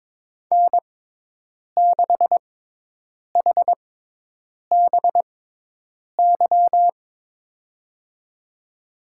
Morse code, heard as a keyed tone, about 22 wpm, spaced as for 8 wpm, 700 Hz. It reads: N6HBY
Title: USA Call Signs - 6x Character Spacing - 22wpm